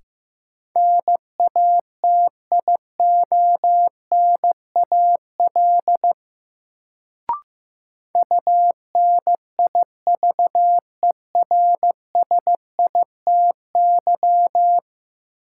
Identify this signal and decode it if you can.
Morse code, heard as a keyed tone, about 15 wpm, 700 Hz. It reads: NATIONAL UNIVERSITY
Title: Sentences from Top 1000 Words - Review Part 4 - New Word Drill - 15wpm